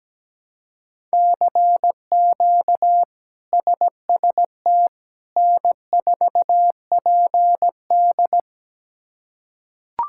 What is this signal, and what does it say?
Morse code, heard as a keyed tone, about 17 wpm, 700 Hz. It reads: CQ SST N4PD
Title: K1USN SST QSO Practice - 17wpm